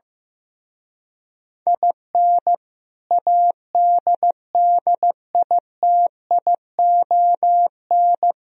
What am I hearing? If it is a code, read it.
Morse code, heard as a keyed tone, about 16 wpm, 700 Hz. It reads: IN ADDITION
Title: Speed-Racing - Sets of 3 Words 15wpm